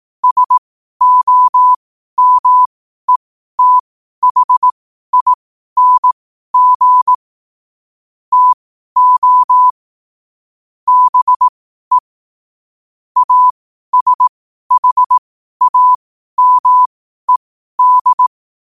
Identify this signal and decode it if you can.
Morse code, heard as a keyed tone, about 18 wpm, spaced as for 12 wpm, 1000 Hz. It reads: SOMETHING TO BE ASHAMED
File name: qotdfiles / 12wpm